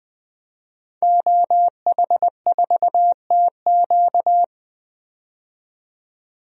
Morse code, heard as a keyed tone, about 20 wpm, 700 Hz. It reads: OH4TQ
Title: DXCC-100 Call Signs - 5 Characters - 20wpm